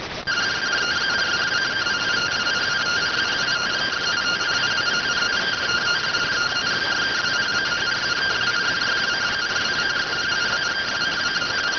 Two-tone teleprinter signal, about 170 Hz shift.